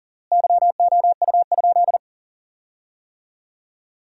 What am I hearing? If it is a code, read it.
Morse code, heard as a keyed tone, about 40 wpm, 700 Hz. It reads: YOU?
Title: 4x Word-Spacing - Sentences from Top 400 Words - 40wpm